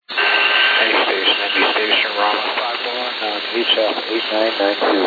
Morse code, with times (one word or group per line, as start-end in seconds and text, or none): none